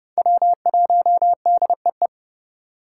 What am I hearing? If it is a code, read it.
Morse code, heard as a keyed tone, about 32 wpm, 700 Hz. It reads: W1DEE
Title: USA Call Signs - 2x Character Spacing - 30wpm